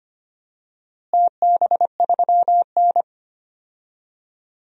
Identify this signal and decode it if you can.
Morse code, heard as a keyed tone, about 25 wpm, 700 Hz. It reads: TB3N
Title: DXCC-100 Call Signs - 4 Characters - Repeated 2x - 25wpm